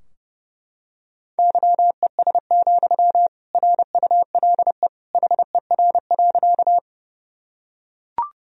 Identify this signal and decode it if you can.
Morse code, heard as a keyed tone, about 29 wpm, 700 Hz. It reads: YES, RULE HER.
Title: Sentences from Top 1000 Words - Review Part 2 - 30wpm